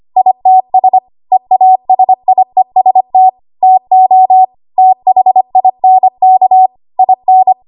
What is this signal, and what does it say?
Morse code, heard as a keyed tone, about 25 wpm, 750 Hz. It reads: ITS EASIEST TO THINK IN
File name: Archive / 25 WPM / 25WPM